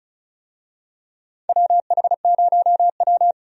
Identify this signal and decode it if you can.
Morse code, heard as a keyed tone, about 34 wpm, 700 Hz. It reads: WH0W
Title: USA Call Signs - 4 Characters - 35wpm